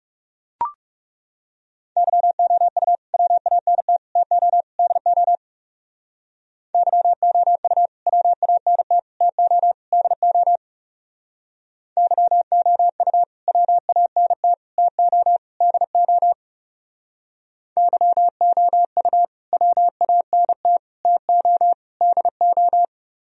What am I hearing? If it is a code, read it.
Morse code, heard as a keyed tone, about 36 wpm, 690 Hz. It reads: YOU WANT TO DO YOU WANT TO DO YOU WANT TO DO YOU WANT TO DO
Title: Speed-Racing - Sets of 4 Words 30wpm